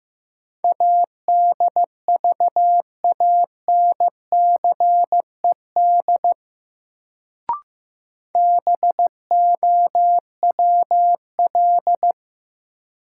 Morse code, heard as a keyed tone, about 15 wpm, 700 Hz. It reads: ADVANCED BOWL
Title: Sentences from Top 1200 Words - New Word Drill Alt - 15wpm